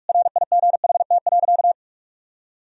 Morse code, heard as a keyed tone, about 46 wpm, 700 Hz. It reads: RIGHT.
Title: Sentences from Top 1000 Words - Review Part 4 - 45wpm